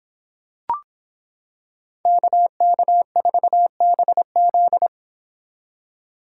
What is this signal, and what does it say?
Morse code, heard as a keyed tone, about 26 wpm, 700 Hz. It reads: KK4BZ